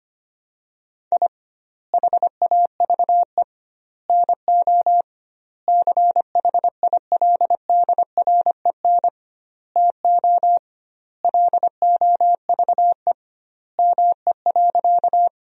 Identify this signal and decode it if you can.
Morse code, heard as a keyed tone, about 25 wpm, 700 Hz. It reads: I HAVE NO CHILDREN TO LOVE ME.
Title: 2x Word-Spacing - Sentences from Top 400 Words - 25wpm